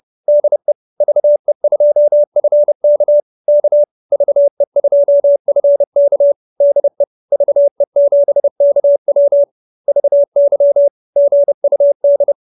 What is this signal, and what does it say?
Morse code, heard as a keyed tone, about 30 wpm, 570 Hz. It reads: DE VE2FK K VE2FK DE VE7KW VY GUD
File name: QSO314 30